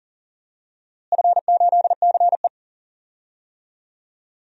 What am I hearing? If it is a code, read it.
Morse code, heard as a keyed tone, about 40 wpm, 700 Hz. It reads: F8CE
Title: DXCC-100 Call Signs - 4 Characters - Repeated 3x - 40wpm